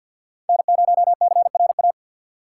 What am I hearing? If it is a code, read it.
Morse code, heard as a keyed tone, about 47 wpm, 700 Hz. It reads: N0CRA
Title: USA Call Signs - 4x Character Spacing - 50wpm